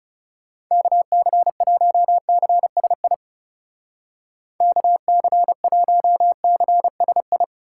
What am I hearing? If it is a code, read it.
Morse code, heard as a keyed tone, about 36 wpm, 700 Hz. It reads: KC1CSI KC1CSI
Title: Speed Racing - USA 2x3 Call Signs 30wpm